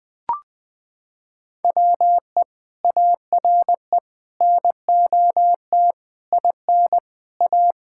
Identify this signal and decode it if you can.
Morse code, heard as a keyed tone, about 20 wpm, 700 Hz. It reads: WE ARE NOT IN A
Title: Sentences with 7-Word Phrases - 20wpm